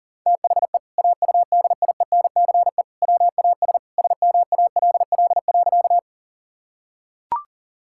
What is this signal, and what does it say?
Morse code, heard as a keyed tone, about 37 wpm, 700 Hz. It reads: THE AUDIENCE WAS SMALL.
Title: Sentences from Top 1000 Words - Review Part 5 - 40wpm